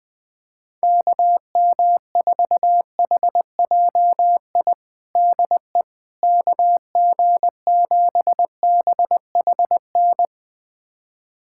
Morse code, heard as a keyed tone, about 20 wpm, 700 Hz. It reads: KM4HJI DE KG7BHN
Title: USA Callsign DE Callsign - 6 Characters - 20wpm